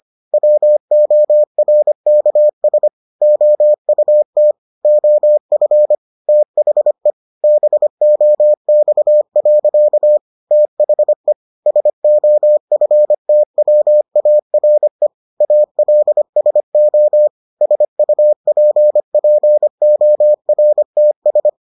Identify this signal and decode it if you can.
Morse code, heard as a keyed tone, about 25 wpm, 600 Hz. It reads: WORKS OUT OF THE BOX. THE SOFTWARE ALSO SUPPORTS